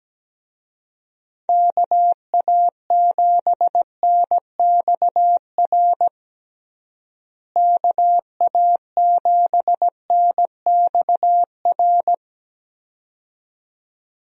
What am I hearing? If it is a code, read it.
Morse code, heard as a keyed tone, about 17 wpm, 700 Hz. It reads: KA7NXR KA7NXR
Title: USA Call Signs - 6 Characters - Repeated 2x - 17wpm